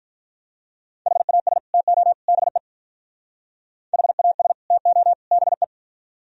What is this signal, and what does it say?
Morse code, heard as a keyed tone, about 46 wpm, 700 Hz. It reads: HAS TO BE HAS TO BE